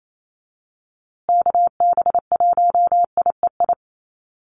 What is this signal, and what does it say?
Morse code, heard as a keyed tone, about 28 wpm, 700 Hz. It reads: KB1IEI
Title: USA Call Signs Repeated 2x 28wpm